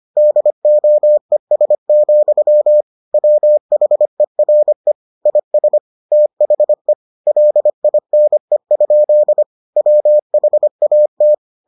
Morse code, heard as a keyed tone, about 25 wpm, 600 Hz. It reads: DOES, WHERE IS THE LINE? WHAT